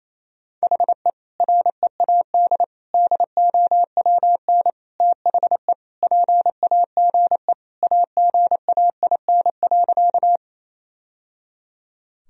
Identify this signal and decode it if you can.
Morse code, heard as a keyed tone, about 27 wpm, 700 Hz. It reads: HE READ DOWN THE PAGE AGAIN.
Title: Sentences from Top 1000 Words - Review Part 1 - 28wpm